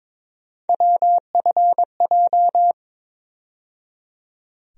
Morse code, heard as a keyed tone, about 22 wpm, 700 Hz. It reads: WFJ
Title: USA Call Sign Postfixes - 3 Letters 22wpm